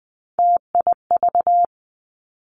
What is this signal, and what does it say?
Morse code, heard as a keyed tone, about 20 wpm, 700 Hz. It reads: TIV
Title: Three Challenging Letter-Number Combinations 20wpm